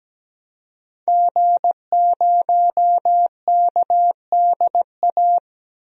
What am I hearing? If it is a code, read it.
Morse code, heard as a keyed tone, about 17 wpm, 700 Hz. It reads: G0KDA
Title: DXCC-100 Call Signs - 5 Characters - 17wpm